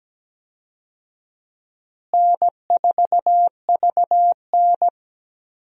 Morse code, heard as a keyed tone, about 17 wpm, 700 Hz. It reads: N4VN
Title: USA Call Signs - 4 Characters - Repeated 2x - 17wpm